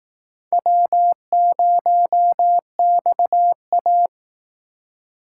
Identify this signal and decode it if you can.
Morse code, heard as a keyed tone, about 18 wpm, 700 Hz. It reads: W0XA
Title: Speed Racing - USA 1x2 Call Signs 15wpm